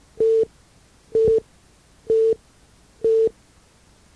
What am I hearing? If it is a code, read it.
There is no code.